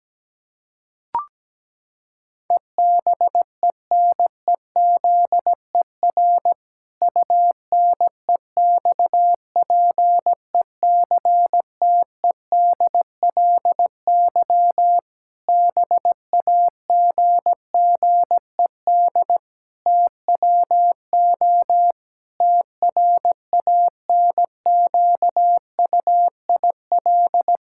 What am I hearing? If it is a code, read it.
Morse code, heard as a keyed tone, about 17 wpm, 700 Hz. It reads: EBENEZER UNEXPECTEDLY BAGGED TWO TRANQUIL
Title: Pangram Challenge v2 - 17wpm